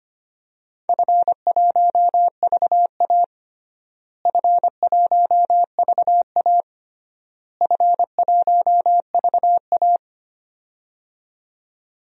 Morse code, heard as a keyed tone, about 25 wpm, 700 Hz. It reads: F1VA F1VA F1VA
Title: DXCC-100 Call Signs - 4 Characters - Repeated 3x - 25wpm